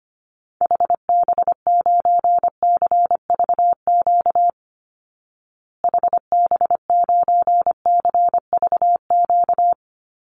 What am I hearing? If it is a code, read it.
Morse code, heard as a keyed tone, about 25 wpm, 700 Hz. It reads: HB9CVQ HB9CVQ
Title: Call Signs Repeated 2x - 25wpm